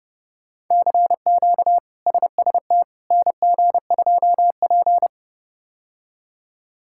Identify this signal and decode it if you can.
Morse code, heard as a keyed tone, about 31 wpm, 700 Hz. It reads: CQ SST NG2P
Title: K1USN SST QSO Practice - 30wpm